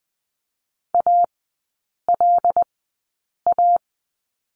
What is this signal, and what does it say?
Morse code, heard as a keyed tone, about 20 wpm, spaced as for 8 wpm, 700 Hz. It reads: ALA
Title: 074 Lesson 13e - Letter L - Three Characters 20wpm